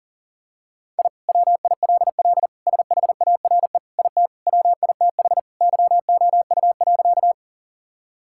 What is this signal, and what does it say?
Morse code, heard as a keyed tone, about 40 wpm, 700 Hz. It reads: I WILL SHARE IT WITH YOU.
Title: Sentences from Top 1000 Words - Review Part 6 - 40wpm